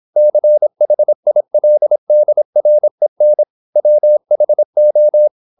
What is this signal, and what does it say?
Morse code, heard as a keyed tone, about 26 wpm, 600 Hz. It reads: CHILDREN WHO
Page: Never 26wpm | CW med Gnister